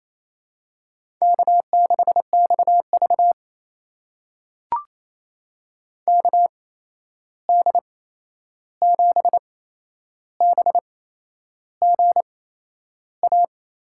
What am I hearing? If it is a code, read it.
Morse code, heard as a keyed tone, about 28 wpm, spaced as for 11 wpm, 700 Hz. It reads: K6XV KD7BGA
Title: USA Call Signs - 8x Character Spacing - 28wpm